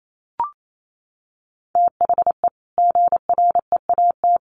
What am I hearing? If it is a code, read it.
Morse code, heard as a keyed tone, about 28 wpm, 700 Hz. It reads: THE GREAT
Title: Sentences from Top 1500 Words 28wpm